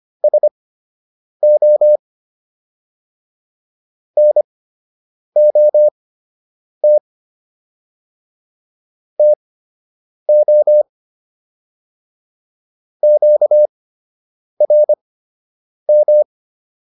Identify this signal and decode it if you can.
Morse code, heard as a keyed tone, about 25 wpm, spaced as for 8 wpm, 600 Hz. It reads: SO NOT TO QRM